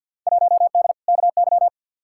Morse code, heard as a keyed tone, about 53 wpm, 700 Hz. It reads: 1D KY
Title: Field Day Exchanges - 50wpm